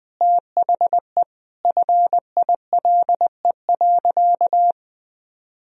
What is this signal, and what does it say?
Morse code, heard as a keyed tone, about 21 wpm, 700 Hz. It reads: THE FILE.
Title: Sentences from Top 1000 Words - Review Part 4 - 20wpm